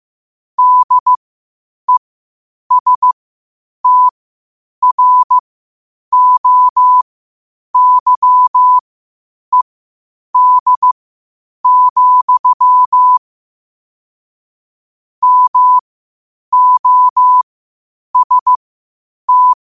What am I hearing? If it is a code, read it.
Morse code, heard as a keyed tone, about 15 wpm, spaced as for 8 wpm, 1000 Hz. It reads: DESTROYED, MOST
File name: qotdfiles / 8wpm